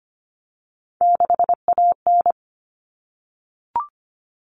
Morse code, heard as a keyed tone, about 25 wpm, 700 Hz. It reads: 6AN
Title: All Cut Numbers - 3-Digit 25wpm